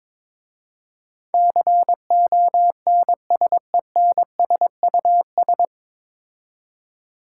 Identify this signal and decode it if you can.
Morse code, heard as a keyed tone, about 22 wpm, 700 Hz. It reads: CONSENSUS
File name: 9 Letter Words - First 3 Letters Given - 22wpm